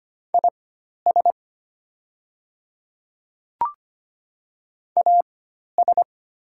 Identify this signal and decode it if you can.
Morse code, heard as a keyed tone, about 25 wpm, spaced as for 12 wpm, 700 Hz. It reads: IS AS
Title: 4x Character Spacing - Top 200 Words - 25wpm